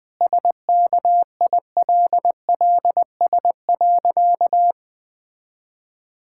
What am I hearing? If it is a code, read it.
Morse code, heard as a keyed tone, about 20 wpm, 700 Hz. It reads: SKILLS.
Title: Difficult Sentences with Hints - 20wpm